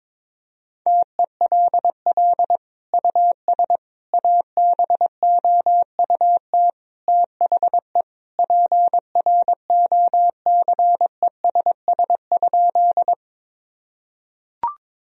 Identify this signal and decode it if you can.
Morse code, heard as a keyed tone, about 22 wpm, 690 Hz. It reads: TELL US ABOUT THE PROCESS?
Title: Sentences from Top 1000 Words - Review Part 3 - 22wpm